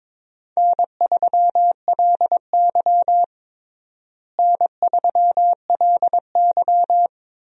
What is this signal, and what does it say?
Morse code, heard as a keyed tone, about 22 wpm, 700 Hz. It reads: N3LY N3LY
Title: USA Call Signs - 4 Characters - Repeated 2x - 22wpm